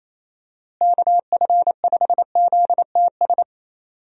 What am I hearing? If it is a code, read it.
Morse code, heard as a keyed tone, about 28 wpm, 700 Hz. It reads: KF5ZTS
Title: USA Call Signs - 6 Characters - Repeated 3x - 28wpm